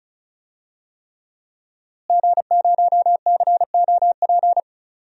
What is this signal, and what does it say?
Morse code, heard as a keyed tone, about 36 wpm, 700 Hz. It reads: G0COP
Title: DXCC-100 Call Signs - 5 Characters - Repeated 3x - 35wpm